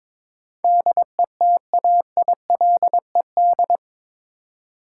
Morse code, heard as a keyed tone, about 22 wpm, 700 Hz. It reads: DETAILED
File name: practice / 22wpm / Eight Letter Words - 22wpm